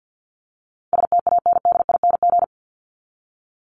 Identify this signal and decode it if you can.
Morse code, heard as a keyed tone, about 50 wpm, 700 Hz. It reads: STANDING
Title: ING Suffix - Top 100 Words with Base Words 50wpm